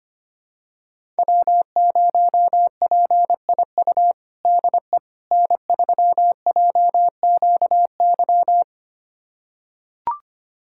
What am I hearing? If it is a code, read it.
Morse code, heard as a keyed tone, about 25 wpm, 700 Hz. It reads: W0PIU DE N3JQY
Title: USA Callsign DE Callsign - 5 Characters - 25wpm